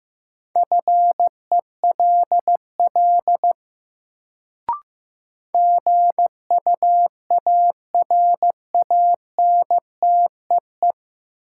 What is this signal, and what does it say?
Morse code, heard as a keyed tone, about 15 wpm, 700 Hz. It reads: FELL GUARANTEE
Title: Sentences from Top 1600 Words - New Word Drill - 15wpm